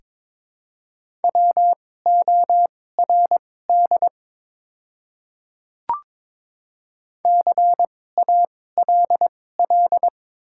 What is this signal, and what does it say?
Morse code, heard as a keyed tone, about 22 wpm, spaced as for 16 wpm, 690 Hz. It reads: WORD CALL
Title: 2x Character Spacing - Top 100 Words - 22wpm